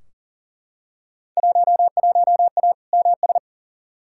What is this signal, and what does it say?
Morse code, heard as a keyed tone, about 37 wpm, 700 Hz. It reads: 11A MS
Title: Field Day Exchanges Encore - 40wpm